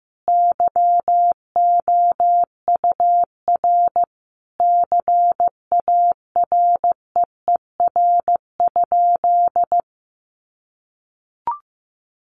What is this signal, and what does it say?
Morse code, heard as a keyed tone, about 15 wpm, 700 Hz. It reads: YOUR CAREER?
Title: Sentences from Top 1000 Words 15wpm